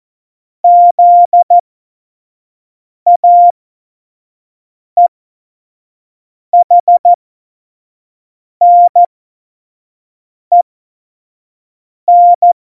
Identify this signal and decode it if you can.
Morse code, heard as a keyed tone, about 14 wpm, spaced as for 5 wpm, 700 Hz. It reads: ZAEHNEN